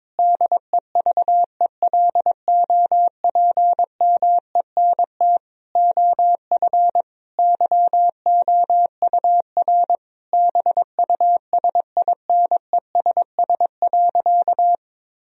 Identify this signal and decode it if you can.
Morse code, heard as a keyed tone, about 22 wpm, 700 Hz. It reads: DEVELOPMENT OF YOUR BUSINESS.
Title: Sentences from Top 1000 Words - Review Part 4 - 22wpm